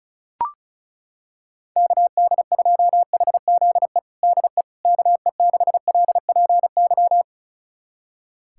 Morse code, heard as a keyed tone, about 36 wpm, 700 Hz. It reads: KD2HZE DE KE6LPY
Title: USA Callsign DE Callsign - 6 Characters - 35wpm